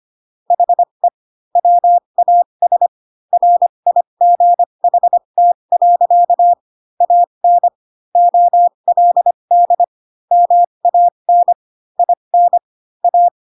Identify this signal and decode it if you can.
Morse code, heard as a keyed tone, about 25 wpm, spaced as for 22 wpm, 700 Hz. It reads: HE WAS RIGHT. AN OLD MAN IN A
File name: Pinocchio / Pinocchio ch6 18 25 ik1hge